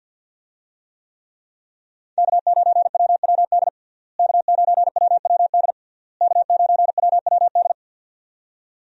Morse code, heard as a keyed tone, about 49 wpm, 700 Hz. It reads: K9WWD K9WWD K9WWD
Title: USA Call Signs - 5 Characters - Repeated 3x - 50wpm